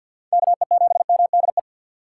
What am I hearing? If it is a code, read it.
Morse code, heard as a keyed tone, about 49 wpm, 700 Hz. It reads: KE7MDE
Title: USA Call Signs - 4x Character Spacing - 50wpm